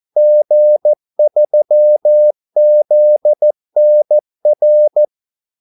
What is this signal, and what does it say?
Morse code, heard as a keyed tone, about 14 wpm, 600 Hz. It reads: G3ZNR
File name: Callsigns-14wpm